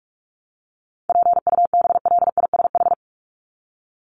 Morse code, heard as a keyed tone, about 44 wpm, 700 Hz. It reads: PUBLISH
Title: ED Suffix - Top 100 Words with Base Words 45wpm